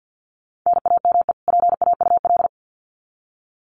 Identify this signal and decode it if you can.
Morse code, heard as a keyed tone, about 50 wpm, 700 Hz. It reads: NAME PAUL
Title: Common QSO Phrases 50wpm